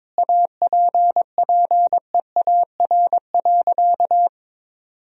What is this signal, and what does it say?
Morse code, heard as a keyed tone, about 21 wpm, 700 Hz. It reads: APPEAR.
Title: Sentences from Top 1000 Words - Review Part 2 - 22wpm